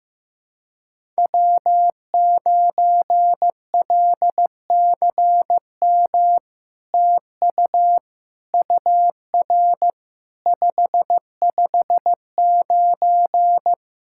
Morse code, heard as a keyed tone, about 15 wpm, 700 Hz. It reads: W9LCM TU UR 559
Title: SOTA QSO Practice 15wpm